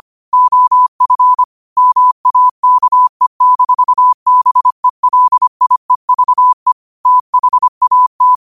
Morse code, heard as a keyed tone, about 25 wpm, 1000 Hz. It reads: OF MAKE-BELIEVE THAT
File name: qotdfiles / 25wpm